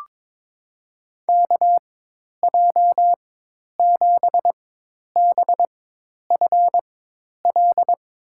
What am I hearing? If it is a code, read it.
Morse code, heard as a keyed tone, about 22 wpm, spaced as for 10 wpm, 700 Hz. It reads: KJ7BFL
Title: USA Call Signs - 4x Character Spacing - 22wpm